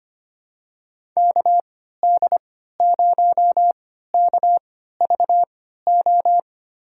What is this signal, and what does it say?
Morse code, heard as a keyed tone, about 25 wpm, spaced as for 14 wpm, 700 Hz. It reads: KD0KVO
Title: USA Call Signs - 3x Character Spacing - 25wpm